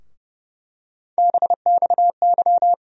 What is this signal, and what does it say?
Morse code, heard as a keyed tone, about 31 wpm, 700 Hz. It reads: BXY
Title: USA Call Sign Postfixes - 3 Letters 30wpm